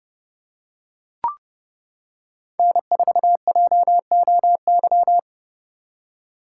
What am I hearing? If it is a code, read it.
Morse code, heard as a keyed tone, about 30 wpm, 700 Hz. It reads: N4JOY